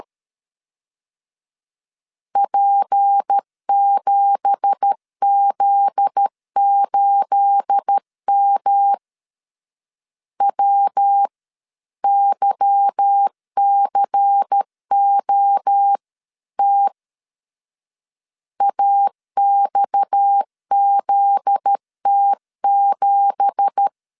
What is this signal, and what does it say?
Morse code, heard as a keyed tone, about 13 wpm, 800 Hz. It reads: P7Z8M WYCOT AXZT7